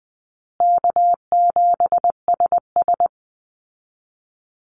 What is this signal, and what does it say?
Morse code, heard as a keyed tone, about 20 wpm, 700 Hz. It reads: K7SS
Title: Easy Call Signs Repeated 2x - 20wpm